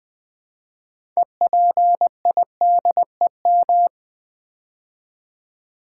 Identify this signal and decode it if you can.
Morse code, heard as a keyed tone, about 20 wpm, 700 Hz. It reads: EPIDEM
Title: Build A Word - 3 Letters At A Time - 8-Letter Words - 20wpm